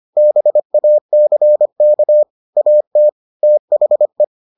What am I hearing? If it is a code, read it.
Morse code, heard as a keyed tone, about 25 wpm, 600 Hz. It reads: BACK AT THE